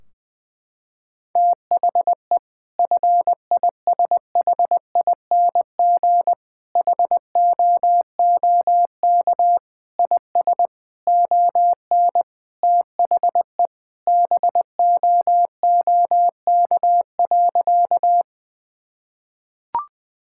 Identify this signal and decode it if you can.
Morse code, heard as a keyed tone, about 20 wpm, 700 Hz. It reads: THE FISHING HOOK IS ON THE BOOK.
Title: Sentences from Top 1600 Words 20wpm